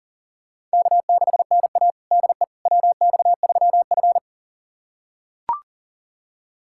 Morse code, heard as a keyed tone, about 40 wpm, 700 Hz. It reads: K6NA DE WX3F
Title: USA Callsign DE Callsign - 4 Characters - 40wpm